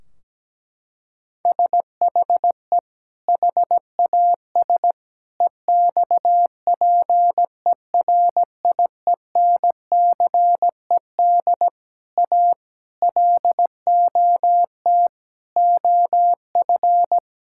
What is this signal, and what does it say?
Morse code, heard as a keyed tone, about 17 wpm, 700 Hz. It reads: SHE HAS EXPERIENCED A LOT OF
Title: Sentences from Top 1600 Words 17wpm